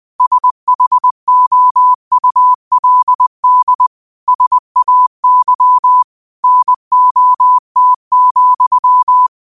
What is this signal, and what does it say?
Morse code, heard as a keyed tone, about 20 wpm, 1000 Hz. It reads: SHOULD SAY NOT,
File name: 20wpm